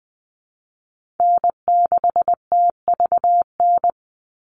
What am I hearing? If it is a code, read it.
Morse code, heard as a keyed tone, about 20 wpm, 700 Hz. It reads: N6TVN
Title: Call Signs Repeated 2x - 20wpm